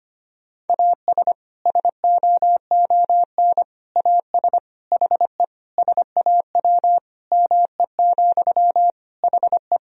Morse code, heard as a keyed tone, about 25 wpm, 700 Hz. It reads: AS SOON AS HE SAW ME, HE
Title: Sentences from Top 1000 Words - Review Part 2 - 25wpm